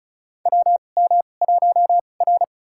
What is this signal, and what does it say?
Morse code, heard as a keyed tone, about 36 wpm, spaced as for 25 wpm, 700 Hz. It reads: WM1R
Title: USA Call Signs - 4 Characters - 2x Character Spacing - 35wpm